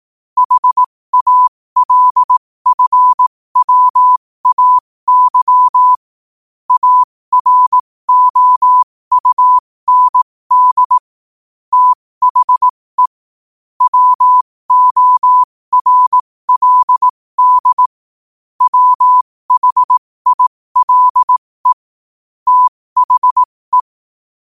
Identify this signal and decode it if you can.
Morse code, heard as a keyed tone, about 18 wpm, spaced as for 15 wpm, 1000 Hz. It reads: HALFWAY AROUND THE WORLD WHILE THE
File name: qotdfiles / 15wpm